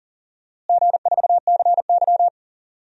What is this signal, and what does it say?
Morse code, heard as a keyed tone, about 40 wpm, 700 Hz. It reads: G4CY
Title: DXCC-100 Call Signs - 4 Characters - Repeated 2x - 40wpm